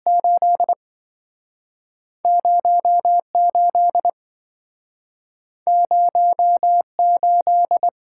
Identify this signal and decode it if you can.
Morse code, heard as a keyed tone, about 27 wpm, spaced as for 5 wpm, 700 Hz. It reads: 80 80 8